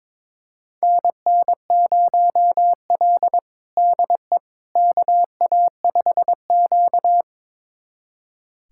Morse code, heard as a keyed tone, about 22 wpm, 700 Hz. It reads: NN0L DE KA5Q
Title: USA Callsign DE Callsign - 4 Characters - 22wpm